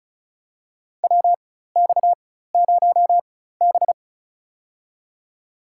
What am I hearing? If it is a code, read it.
Morse code, heard as a keyed tone, about 34 wpm, spaced as for 16 wpm, 700 Hz. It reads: WX0B
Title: USA Call Signs - 4 Characters - 4x Character Spacing - 35wpm